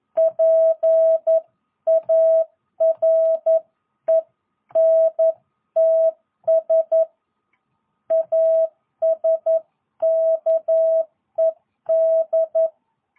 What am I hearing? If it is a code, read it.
Morse code, heard as a keyed tone, about 11 wpm, 650 Hz. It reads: PARENTS ASKED
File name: jh8cbh / 2022cw